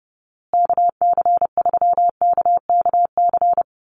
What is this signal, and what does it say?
Morse code, heard as a keyed tone, about 30 wpm, 700 Hz. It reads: KC3KKC
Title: USA Call Signs Repeated 3x 30wpm